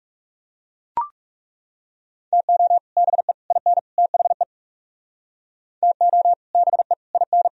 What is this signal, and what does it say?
Morse code, heard as a keyed tone, about 38 wpm, 700 Hz. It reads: TO BE IN THE TO BE IN